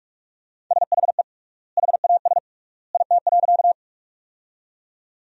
Morse code, heard as a keyed tone, about 41 wpm, 700 Hz. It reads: SHE HAS IT.
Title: 3x Word-Spacing - Sentences from Top 100 Words - 45wpm